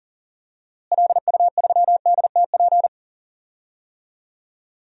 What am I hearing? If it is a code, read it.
Morse code, heard as a keyed tone, about 37 wpm, 700 Hz. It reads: LU3DTP
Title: DXCC-100 Call Signs - 6 Characters - Repeated 3x - 40wpm